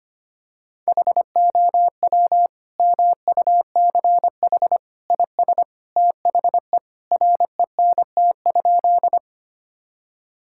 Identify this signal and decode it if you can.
Morse code, heard as a keyed tone, about 25 wpm, 700 Hz. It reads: HOW MUCH IS THE RENT?
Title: Sentences from Top 1000 Words - Review Part 5 - 25wpm